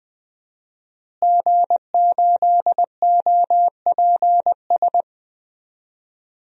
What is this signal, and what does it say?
Morse code, heard as a keyed tone, about 20 wpm, 700 Hz. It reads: G8OPS
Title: DXCC-100 Call Signs - 5 Characters - Repeated 2x - 20wpm